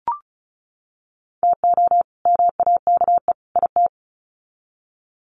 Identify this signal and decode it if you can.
Morse code, heard as a keyed tone, about 34 wpm, 700 Hz. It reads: TO MAKE IT
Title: Sets of 3 Words 35wpm